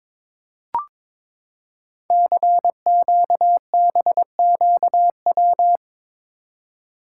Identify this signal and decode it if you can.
Morse code, heard as a keyed tone, about 22 wpm, 700 Hz. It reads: CQBQW
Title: Code Groups - 5 Letter:Numbers - 22wpm